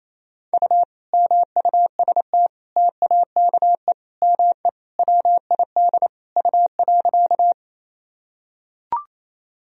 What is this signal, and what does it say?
Morse code, heard as a keyed tone, about 27 wpm, 700 Hz. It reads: U MUST TAKE ME WID U.
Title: Sentences from Top 200 Words and Abbreviations 28wpm